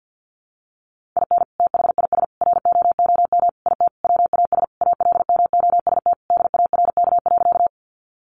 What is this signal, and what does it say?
Morse code, heard as a keyed tone, about 50 wpm, 700 Hz. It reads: IN THIS ROOM IT WAS ALMOST DARK.
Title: Sentences from Top 700 Words 50wpm